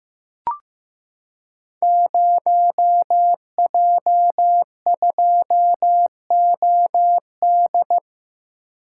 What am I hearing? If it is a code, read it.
Morse code, heard as a keyed tone, about 15 wpm, 700 Hz. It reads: 0J2OD
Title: Code Groups - 5 Letter:Numbers - 15wpm